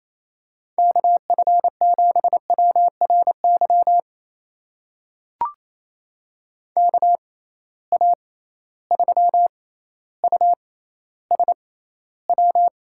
Morse code, heard as a keyed tone, about 28 wpm, spaced as for 14 wpm, 700 Hz. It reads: KF7WRY KA3USW
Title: USA Call Signs - 6 Characters - 6x Character Spacing - 28wpm